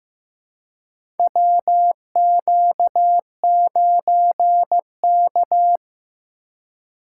Morse code, heard as a keyed tone, about 15 wpm, 700 Hz. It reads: WQ9K
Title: USA Call Signs - 4 Characters - 3x Character Spacing - 15wpm